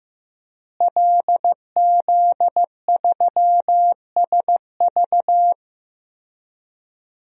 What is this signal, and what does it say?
Morse code, heard as a keyed tone, about 15 wpm, 700 Hz. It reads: LZ3SV
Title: DXCC-100 Call Signs - 5 Characters - Repeated 2x - 15wpm